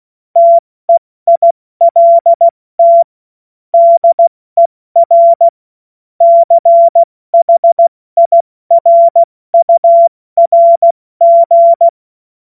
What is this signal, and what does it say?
Morse code, heard as a keyed tone, about 16 wpm, 680 Hz. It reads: TEILT DER CHIRURG
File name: Morsekurs / Witze / Witz 18. Juli 2023 70 F80